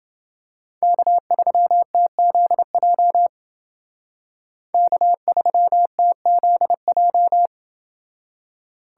Morse code, heard as a keyed tone, about 28 wpm, 700 Hz. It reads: K3TZJ K3TZJ